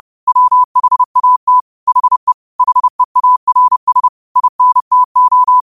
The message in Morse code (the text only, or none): WHAT HE HEARS INTO